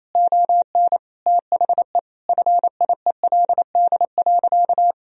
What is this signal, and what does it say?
Morse code, heard as a keyed tone, about 28 wpm, 700 Hz. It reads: ON THE FIELD.
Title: Sentences from Top 1000 Words - Review Part 2 - 28wpm